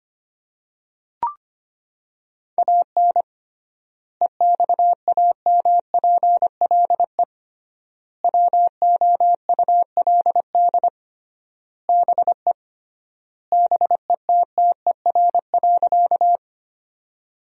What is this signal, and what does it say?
Morse code, heard as a keyed tone, about 25 wpm, 700 Hz. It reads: AN EXAMPLE WOULD BE BETTER.
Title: 3x Word-Spacing - Sentences from Top 400 Words - 25wpm